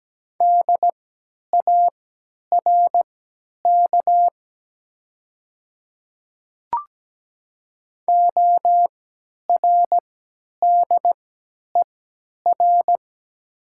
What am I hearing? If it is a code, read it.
Morse code, heard as a keyed tone, about 17 wpm, spaced as for 10 wpm, 700 Hz. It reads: DARK ORDER
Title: 3x Character Spacing - Top 500 Words - 17wpm